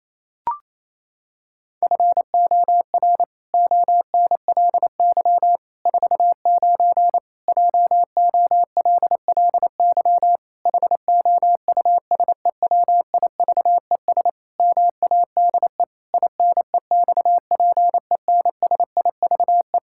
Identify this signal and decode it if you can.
Morse code, heard as a keyed tone, about 28 wpm, 700 Hz. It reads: FOR ONLY 49 JOLLY HOUSEWIVES MADE INEXPENSIVE
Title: Pangram Challenge v2 - 28wpm